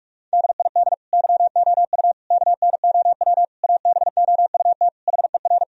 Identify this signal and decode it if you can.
Morse code, heard as a keyed tone, about 46 wpm, 700 Hz. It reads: DID YOU KNOW ABOUT HER
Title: Sentences from Top 1000 Words - Review Part 4 - 45wpm